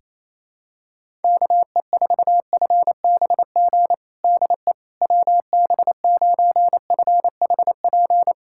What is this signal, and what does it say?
Morse code, heard as a keyed tone, about 28 wpm, 700 Hz. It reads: KE4FBG DE WB9FHP
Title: USA Callsign DE Callsign - 6 Characters - 28wpm